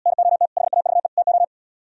Morse code, heard as a keyed tone, about 77 wpm, 700 Hz. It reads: NOT HAVE TO